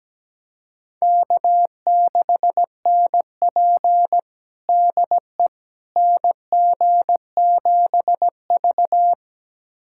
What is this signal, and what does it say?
Morse code, heard as a keyed tone, about 17 wpm, 700 Hz. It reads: K6NP DE NG7V
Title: USA Callsign DE Callsign - 4 Characters - 17wpm